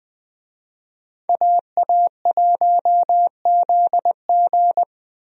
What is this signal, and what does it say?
Morse code, heard as a keyed tone, about 21 wpm, 700 Hz. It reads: AA1ZG
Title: USA Call Signs - 5 Characters - Repeated 3x - 20wpm